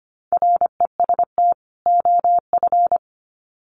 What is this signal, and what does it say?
Morse code, heard as a keyed tone, about 25 wpm, 700 Hz. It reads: REST OF
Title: Sets of 2 Words 25wpm